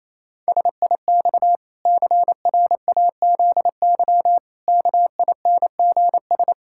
Morse code, heard as a keyed tone, about 28 wpm, 700 Hz. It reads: SIX CRAZY KINGS
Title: Pangram Challenge v2 - 28wpm